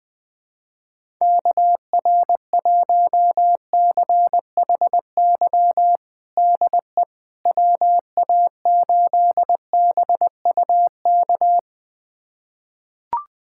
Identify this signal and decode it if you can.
Morse code, heard as a keyed tone, about 20 wpm, 700 Hz. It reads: KR1CHY DE WA8BUK
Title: USA Callsign DE Callsign - 6 Characters - 20wpm